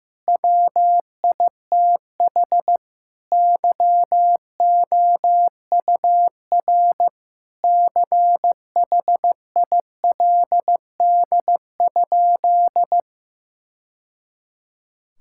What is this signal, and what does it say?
Morse code, heard as a keyed tone, about 16 wpm, 700 Hz. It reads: WITH YOUR CHILD?
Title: Sentences from Top 1000 Words - Review Part 6 - 15wpm